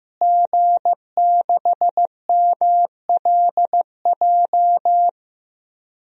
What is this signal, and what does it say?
Morse code, heard as a keyed tone, about 15 wpm, 700 Hz. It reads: G6MLJ
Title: DXCC-100 Call Signs - 5 Characters - Repeated 2x - 15wpm